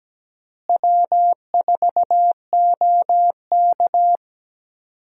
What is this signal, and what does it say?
Morse code, heard as a keyed tone, about 17 wpm, 700 Hz. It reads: W4OK
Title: USA Call Signs - 4 Characters - Repeated 3x - 17wpm